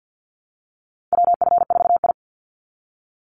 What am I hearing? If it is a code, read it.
Morse code, heard as a keyed tone, about 50 wpm, 700 Hz. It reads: WF4I
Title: NA SOTA Chaser Callsigns 50wpm